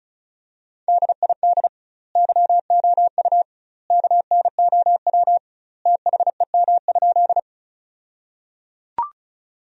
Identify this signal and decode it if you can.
Morse code, heard as a keyed tone, about 34 wpm, 700 Hz. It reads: DID YOU KNOW THEM?
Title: 2x Word-Spacing - Sentences from Top 100 Words - 35wpm